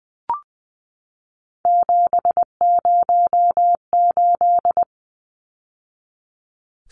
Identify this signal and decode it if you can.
Morse code, heard as a keyed tone, about 20 wpm, 700 Hz. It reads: 708
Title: Numbers - 3-Digit 20wpm